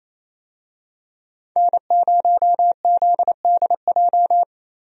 Morse code, heard as a keyed tone, about 27 wpm, 700 Hz. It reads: N0ZDJ
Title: USA Call Signs - 5 Characters - Repeated 2x - 28wpm